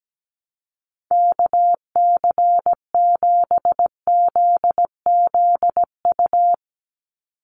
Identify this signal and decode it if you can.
Morse code, heard as a keyed tone, about 17 wpm, 700 Hz. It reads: KC7ZZU
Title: USA Call Signs 17wpm